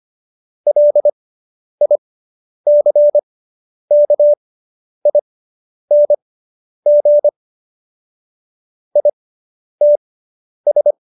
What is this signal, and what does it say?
Morse code, heard as a keyed tone, about 25 wpm, spaced as for 10 wpm, 590 Hz. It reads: LICKING ITS